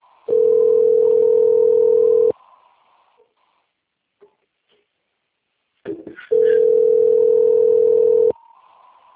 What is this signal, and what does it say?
Ringback tone (North America), 440 + 480 Hz.
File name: rg-1600-unknown- - - .15026